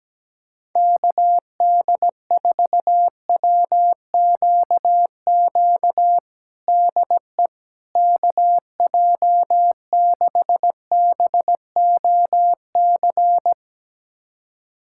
Morse code, heard as a keyed tone, about 17 wpm, 700 Hz. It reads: KD4WQQ DE KJ6BOC
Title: USA Callsign DE Callsign - 6 Characters - 17wpm